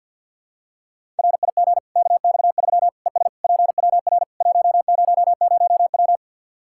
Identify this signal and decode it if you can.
Morse code, heard as a keyed tone, about 46 wpm, 700 Hz. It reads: RIG KX3 ES PWR 100W
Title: Practice QSO - Exchange 2 - Info - 50wpm